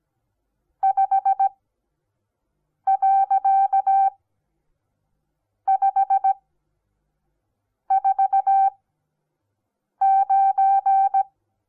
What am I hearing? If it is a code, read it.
Morse code, heard as a keyed tone, about 17 wpm, spaced as for 5 wpm, 780 Hz. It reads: 5.549
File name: Onlinelehrgang / morsen / DJ4UF Morsen08